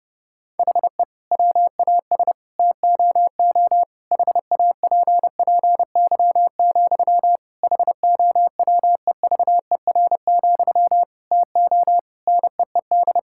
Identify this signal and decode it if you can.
Morse code, heard as a keyed tone, about 30 wpm, 700 Hz. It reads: HE WAS TOO HAPPY, HOWEVER, TO NEED
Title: Sentences from Top 1000 Words - Review Part 6 - 30wpm